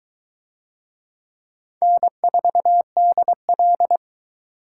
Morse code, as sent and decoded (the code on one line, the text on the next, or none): -. ....- -.. .-..
N4DL